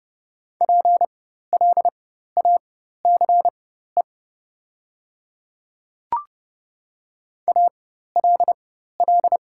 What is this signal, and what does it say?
Morse code, heard as a keyed tone, about 30 wpm, spaced as for 14 wpm, 700 Hz. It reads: PLACE ALL
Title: 4x Character Spacing - Top 200 Words - 30wpm